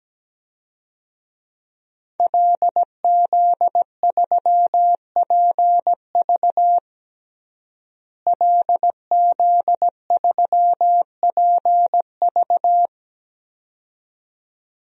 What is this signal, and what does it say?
Morse code, heard as a keyed tone, about 17 wpm, 700 Hz. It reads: LZ3PV LZ3PV
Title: DXCC-100 Call Signs - 5 Characters - Repeated 2x - 17wpm